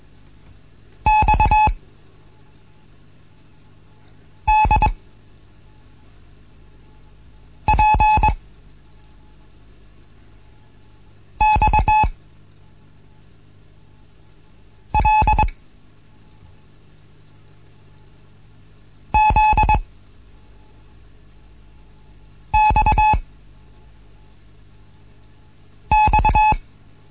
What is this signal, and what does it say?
Morse code, heard as a keyed tone, about 21 wpm, 860 Hz. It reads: X D P X L Z X X
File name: Morsen / Kurs A 15